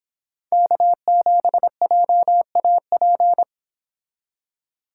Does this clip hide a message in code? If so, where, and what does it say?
Yes: Morse code from 0.5 to 3.4 s, heard as a keyed tone, about 25 wpm, 700 Hz. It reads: K7JAP